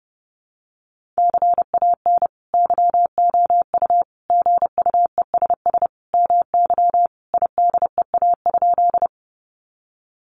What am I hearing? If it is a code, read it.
Morse code, heard as a keyed tone, about 30 wpm, 700 Hz. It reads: CAN YOU GUESS MY IDEA?
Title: Sentences from Top 1500 Words 30wpm